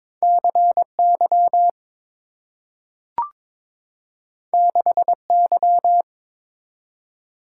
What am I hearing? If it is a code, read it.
Morse code, heard as a keyed tone, about 22 wpm, 700 Hz. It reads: CY 6Y
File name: practice / 22wpm / Two Letter-Number Combinations - 22wpm